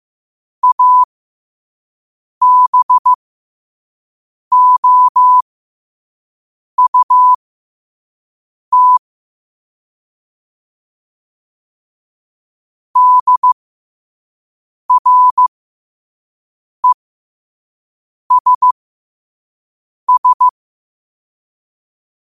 Morse code, heard as a keyed tone, about 15 wpm, spaced as for 5 wpm, 1000 Hz. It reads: ABOUT DRESS
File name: qotdfiles / 5wpm